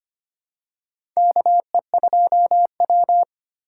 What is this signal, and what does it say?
Morse code, heard as a keyed tone, about 25 wpm, 700 Hz. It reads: KE2W
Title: USA Call Signs - 3x Character Spacing - 25wpm